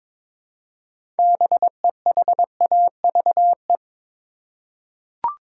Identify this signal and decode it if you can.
Morse code, heard as a keyed tone, about 22 wpm, 700 Hz. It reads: BEHAVE
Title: Six Letter Words - 22wpm